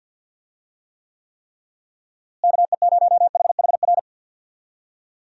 Morse code, heard as a keyed tone, about 49 wpm, 700 Hz. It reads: KE0HHR
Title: USA Call Signs - 6 Characters - Repeated 3x - 50wpm